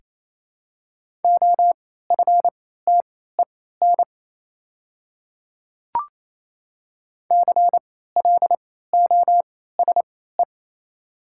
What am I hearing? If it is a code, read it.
Morse code, heard as a keyed tone, about 27 wpm, spaced as for 16 wpm, 700 Hz. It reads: OFTEN CLOSE
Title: 3x Character Spacing - Top 300 Words - 28wpm